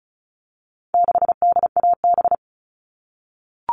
Morse code, heard as a keyed tone, about 34 wpm, 700 Hz. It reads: 6DAB
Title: All Cut Numbers - 4-Digit 35wpm